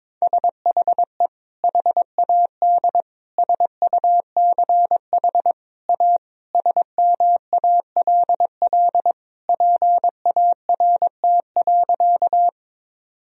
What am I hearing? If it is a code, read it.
Morse code, heard as a keyed tone, about 22 wpm, 700 Hz. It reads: SHE HAD SUCH A SMALL PART.
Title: Sentences from Top 1000 Words - Review Part 1 - 22wpm